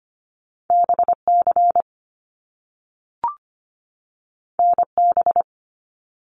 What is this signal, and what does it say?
Morse code, heard as a keyed tone, about 25 wpm, 700 Hz. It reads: BC NB
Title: Canadian Provinces and Territories 25wpm